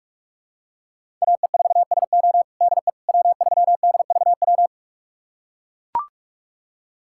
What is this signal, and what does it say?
Morse code, heard as a keyed tone, about 41 wpm, 700 Hz. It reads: AE4SO DE W3DVW
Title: USA Callsign DE Callsign - 5 Characters - 45wpm